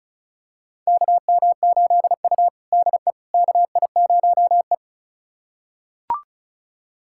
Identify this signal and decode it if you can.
Morse code, heard as a keyed tone, about 36 wpm, 700 Hz. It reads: KM8U DE KI0E
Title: USA Callsign DE Callsign - 4 Characters - 35wpm